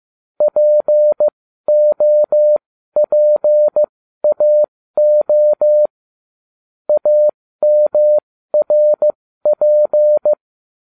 Morse code, heard as a keyed tone, about 15 wpm, spaced as for 12 wpm, 600 Hz. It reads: POPAO AMRP
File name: cw / ubacw09 1